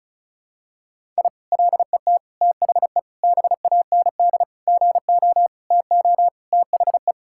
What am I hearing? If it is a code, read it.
Morse code, heard as a keyed tone, about 34 wpm, 700 Hz. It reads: I LET THE BAND GO TO THE
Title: Sentences from Top 1600 Words 35wpm